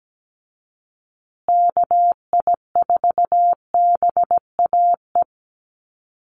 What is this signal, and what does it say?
Morse code, heard as a keyed tone, about 17 wpm, 700 Hz. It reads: KI4BAE
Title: USA Call Signs Repeated 2x 17wpm